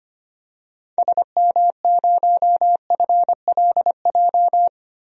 Morse code, heard as a keyed tone, about 25 wpm, 700 Hz. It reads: SM0FLJ
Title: DXCC-100 Call Signs - 6 Characters - 25wpm